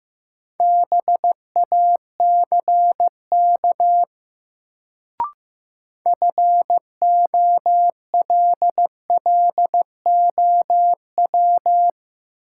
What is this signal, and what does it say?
Morse code, heard as a keyed tone, about 15 wpm, 700 Hz. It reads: BACK FOLLOW
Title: Sentences from Top 200 Words - New Word Drill - 15wpm